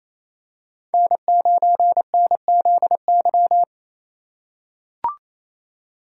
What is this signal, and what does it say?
Morse code, heard as a keyed tone, about 28 wpm, 700 Hz. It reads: N9NZY
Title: USA Call Signs - 5 Characters - 2x Character Spacing - 28wpm